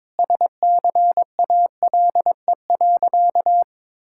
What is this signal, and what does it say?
Morse code, heard as a keyed tone, about 21 wpm, 700 Hz. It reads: SCALE.
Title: Sentences from Top 1000 Words - Review Part 5 - 22wpm